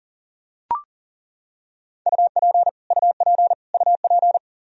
Morse code, heard as a keyed tone, about 37 wpm, 700 Hz. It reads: UP UP UP
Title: Top 100 Words Repeated 40wpm